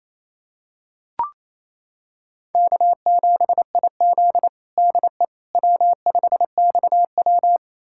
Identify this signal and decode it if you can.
Morse code, heard as a keyed tone, about 28 wpm, 700 Hz. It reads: K7IZ DE W5XW
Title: USA Callsign DE Callsign - 4 Characters - 28wpm